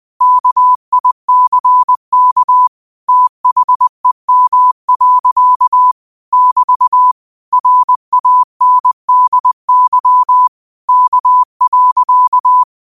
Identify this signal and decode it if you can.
Morse code, heard as a keyed tone, about 20 wpm, 1000 Hz. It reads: KICK THEM. <BT> RANDY K.
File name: qotdfiles / 20wpm